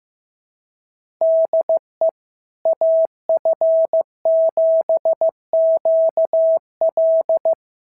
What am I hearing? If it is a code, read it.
Morse code, heard as a keyed tone, about 16 wpm, 650 Hz. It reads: DE AF7QL
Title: SOTA QSO Practice 15wpm